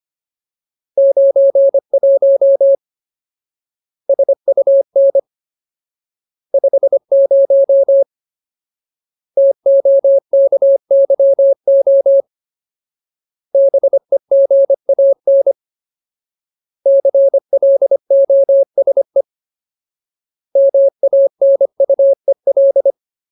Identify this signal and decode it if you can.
Morse code, heard as a keyed tone, about 25 wpm, 550 Hz. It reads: 91 SUN 50 TOKYO BEGAN CLOSE MANUEL
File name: Ultimate Word List / 25WPM Ultimate